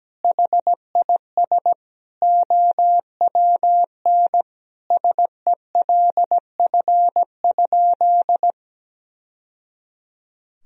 Morse code, heard as a keyed tone, about 17 wpm, 700 Hz. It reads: HIS OWN SELF?
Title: Sentences from Top 1000 Words - Review Part 1 - 17wpm